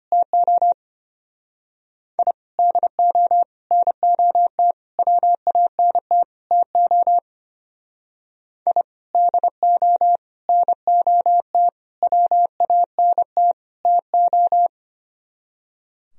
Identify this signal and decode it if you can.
Morse code, heard as a keyed tone, about 32 wpm, 700 Hz. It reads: TO I DO NOT WANT TO I DO NOT WANT TO